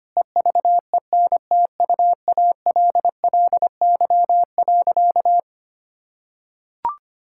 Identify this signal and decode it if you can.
Morse code, heard as a keyed tone, about 25 wpm, 700 Hz. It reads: EVENTUALLY.
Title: Sentences from Top 1000 Words - Review Part 6 - 25wpm